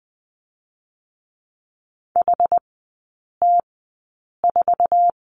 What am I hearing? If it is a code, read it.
Morse code, heard as a keyed tone, about 20 wpm, spaced as for 8 wpm, 700 Hz. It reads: HT4
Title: 184 Lesson 30e - Letter b - Three Characters 20wpm